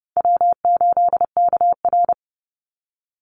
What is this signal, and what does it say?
Morse code, heard as a keyed tone, about 30 wpm, 700 Hz. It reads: W8KR
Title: Easy Call Signs - 30wpm